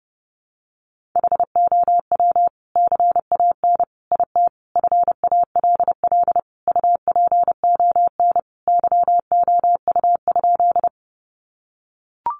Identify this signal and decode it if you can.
Morse code, heard as a keyed tone, about 30 wpm, 700 Hz. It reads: HOW CAN IT FALL UPON YOU?
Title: Sentences from Top 1100 Words 30wpm